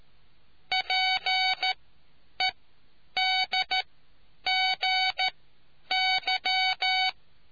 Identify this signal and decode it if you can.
Morse code, heard as a keyed tone, about 13 wpm, spaced as for 9 wpm, 750 Hz. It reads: PEDGY